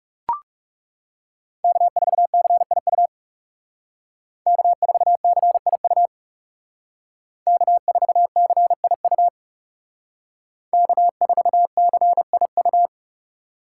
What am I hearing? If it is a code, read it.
Morse code, heard as a keyed tone, about 38 wpm, 700 Hz. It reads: K4CIU K4CIU K4CIU K4CIU